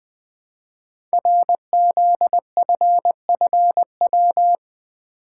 Morse code, heard as a keyed tone, about 20 wpm, 700 Hz. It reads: RZFFW
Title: Code Groups - 5 Letters - 20wpm